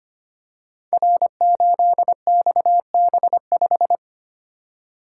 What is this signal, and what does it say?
Morse code, heard as a keyed tone, about 25 wpm, 700 Hz. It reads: R8XB5
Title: Code Groups - 5 Letter:Numbers - 25wpm